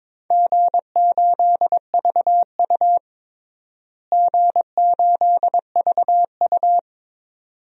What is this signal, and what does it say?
Morse code, heard as a keyed tone, about 22 wpm, 700 Hz. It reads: G8VU G8VU
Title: DXCC-100 Call Signs - 4 Characters - Repeated 2x - 22wpm